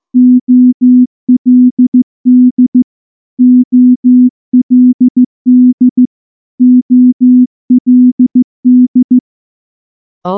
Morse code, heard as a keyed tone, about 14 wpm, 250 Hz. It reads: OLD OLD OLD
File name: morse-training-15WPM